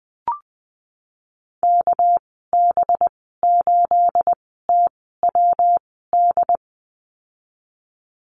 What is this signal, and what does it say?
Morse code, heard as a keyed tone, about 20 wpm, spaced as for 15 wpm, 700 Hz. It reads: KB8TWD
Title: 219 Lesson 35g - Number 0 - Call Signs 20wpm